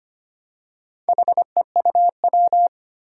Morse code, heard as a keyed tone, about 25 wpm, 700 Hz. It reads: HEUW
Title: Four Challenging Letter-Number Combinations 25wpm